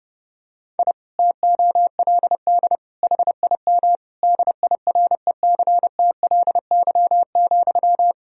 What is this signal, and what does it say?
Morse code, heard as a keyed tone, about 32 wpm, 700 Hz. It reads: I TOLD HIM DIRECTLY,
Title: Sentences from Top 1000 Words - Review Part 1 - 30wpm